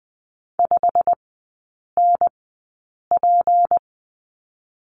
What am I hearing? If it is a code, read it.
Morse code, heard as a keyed tone, about 20 wpm, spaced as for 8 wpm, 690 Hz. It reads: 5NP
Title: 159 Lesson 26e - Letter g - Three Characters 20wpm